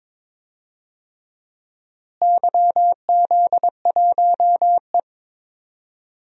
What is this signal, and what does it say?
Morse code, heard as a keyed tone, about 23 wpm, 700 Hz. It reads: YZ1E
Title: DXCC-100 Call Signs - 4 Characters - Repeated 3x - 22wpm